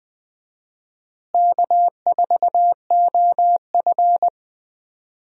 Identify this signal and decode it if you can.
Morse code, heard as a keyed tone, about 21 wpm, 700 Hz. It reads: K4OF
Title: USA Call Signs - 4 Characters - Repeated 2x - 20wpm